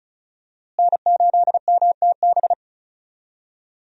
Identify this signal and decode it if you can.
Morse code, heard as a keyed tone, about 36 wpm, 700 Hz. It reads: N8MTB
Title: USA Call Signs - 5 Characters - Repeated 3x - 35wpm